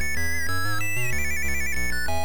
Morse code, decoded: none